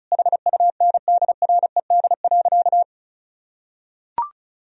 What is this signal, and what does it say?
Morse code, heard as a keyed tone, about 36 wpm, 700 Hz. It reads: HUNDRED.
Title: Sentences from Top 1000 Words - Review Part 2 - 35wpm